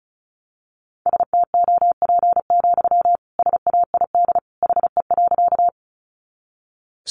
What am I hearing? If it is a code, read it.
Morse code, heard as a keyed tone, about 34 wpm, 700 Hz. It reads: STOP, SAID HE.
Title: Sentences from Top 300 Words 35wpm